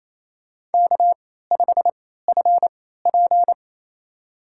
Morse code, heard as a keyed tone, about 28 wpm, spaced as for 16 wpm, 700 Hz. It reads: K5FP
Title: USA Call Signs - 4 Characters - 3x Character Spacing - 28wpm